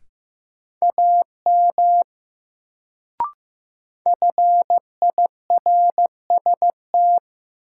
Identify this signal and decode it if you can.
Morse code, heard as a keyed tone, about 16 wpm, 700 Hz. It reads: AM FIRST
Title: Sets of 2 Words - New Word Drill - 15wpm